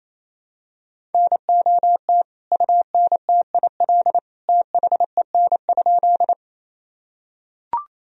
Morse code, heard as a keyed tone, about 28 wpm, 700 Hz. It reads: NOT UNTIL THEN?
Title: Sentences from Top 1000 Words - Review Part 1 - 28wpm